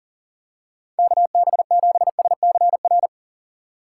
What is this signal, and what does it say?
Morse code, heard as a keyed tone, about 40 wpm, 700 Hz. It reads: KB7SCR
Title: USA Call Signs - 6 Characters - 4x Character Spacing - 40wpm